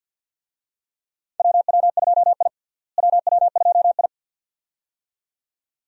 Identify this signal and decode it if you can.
Morse code, heard as a keyed tone, about 51 wpm, 700 Hz. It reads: WW2I WW2I
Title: USA Call Signs - 4 Characters - Repeated 2x - 50wpm